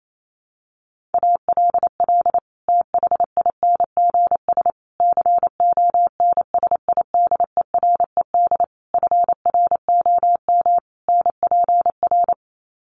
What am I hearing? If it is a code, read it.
Morse code, heard as a keyed tone, about 28 wpm, 700 Hz. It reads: ALL THINGS CONSIDERED FROM NPR
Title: Sets of 5 Words Encore 28wpm